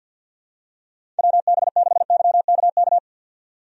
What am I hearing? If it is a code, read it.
Morse code, heard as a keyed tone, about 53 wpm, 700 Hz. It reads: WB6YKK
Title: USA Call Signs - 6 Characters - 50wpm